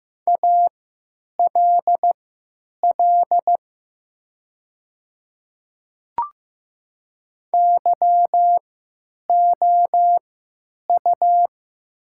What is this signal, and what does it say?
Morse code, heard as a keyed tone, about 15 wpm, spaced as for 9 wpm, 700 Hz. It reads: ALL YOU
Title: 3x Character Spacing - Top 300 Words - 15wpm